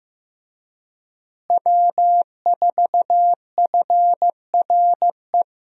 Morse code, heard as a keyed tone, about 15 wpm, 700 Hz. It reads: W4FRE
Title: USA Call Signs - 5 Characters - Repeated 3x - 15wpm